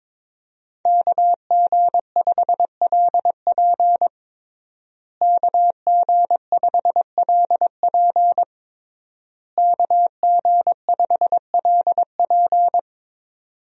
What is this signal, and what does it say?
Morse code, heard as a keyed tone, about 22 wpm, 700 Hz. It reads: KG5LP KG5LP KG5LP
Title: USA Call Signs - 5 Characters - Repeated 3x - 22wpm